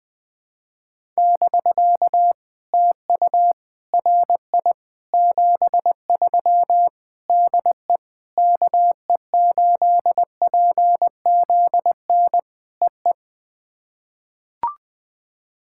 Morse code, heard as a keyed tone, about 20 wpm, 700 Hz. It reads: <BK> TU RI 73 DE KE8PZN EE
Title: POTA QSO Practice - 20wpm